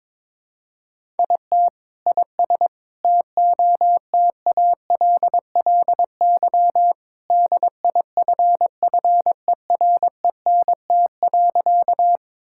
Morse code, heard as a keyed tone, about 22 wpm, 700 Hz. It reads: IT IS TOTALLY DIFFERENT.
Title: Sentences from Top 1600 Words 22wpm